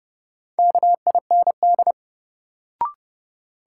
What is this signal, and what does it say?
Morse code, heard as a keyed tone, about 30 wpm, 700 Hz. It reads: KIND
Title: Sets of 2 Words - New Word Drill - 30wpm